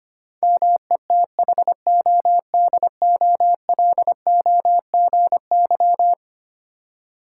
Morse code, heard as a keyed tone, about 25 wpm, 700 Hz. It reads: METHODOLOGY
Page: morse-code-ninja-practice-25wpm